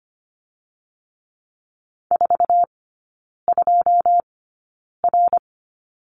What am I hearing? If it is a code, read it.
Morse code, heard as a keyed tone, about 25 wpm, spaced as for 9 wpm, 700 Hz. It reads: 42R
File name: course / 25:10wpm / 093 Lesson 16e - Letter c - Three Characters 25wpm